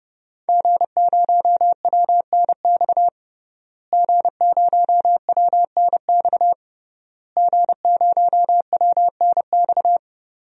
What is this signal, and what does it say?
Morse code, heard as a keyed tone, about 32 wpm, 700 Hz. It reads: G0WNX G0WNX G0WNX
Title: DXCC-100 Call Signs - 5 Characters - Repeated 3x - 30wpm